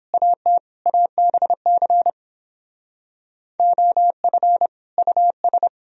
Morse code, heard as a keyed tone, about 29 wpm, 700 Hz. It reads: AT ABC OF US